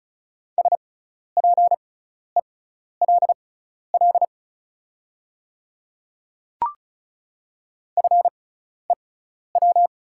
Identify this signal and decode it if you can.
Morse code, heard as a keyed tone, about 36 wpm, spaced as for 12 wpm, 690 Hz. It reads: SPELL FEW
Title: 6x Character Spacing - Top 300 Words - 35wpm